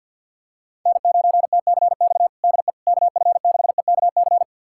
Morse code, heard as a keyed tone, about 46 wpm, 700 Hz. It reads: N9TCX DE KF6EKC
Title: USA Callsign DE Callsign 50wpm